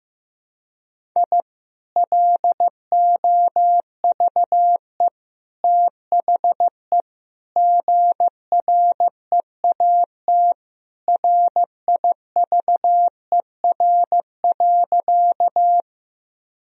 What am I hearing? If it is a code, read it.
Morse code, heard as a keyed tone, about 15 wpm, 700 Hz. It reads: I LOVE THE GREAT RIVER.
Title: Sentences from Top 1000 Words - Review Part 1 - 15wpm